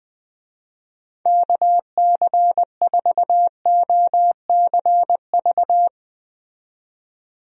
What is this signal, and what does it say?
Morse code, heard as a keyed tone, about 21 wpm, 700 Hz. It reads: KC4OCV
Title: USA Call Signs - 6 Characters - 20wpm